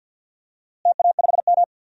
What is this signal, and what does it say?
Morse code, heard as a keyed tone, about 50 wpm, 700 Hz. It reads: TA5M
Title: DXCC-100 Call Signs - 4 Characters - Repeated 2x - 50wpm